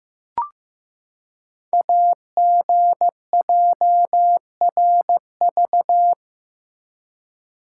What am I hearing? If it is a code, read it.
Morse code, heard as a keyed tone, about 15 wpm, 700 Hz. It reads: AGJRV
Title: Code Groups - 5 Letter:Numbers - 15wpm